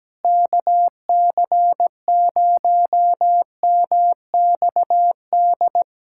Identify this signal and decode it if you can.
Morse code, heard as a keyed tone, about 17 wpm, 700 Hz. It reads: KC0MXD
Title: USA Callsign DE Callsign - 6 Characters - 17wpm